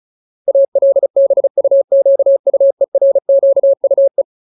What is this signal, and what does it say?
Morse code, heard as a keyed tone, about 34 wpm, 550 Hz. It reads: ALBUQUERQUE
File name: Ultimate Word List / 35WPM Ultimate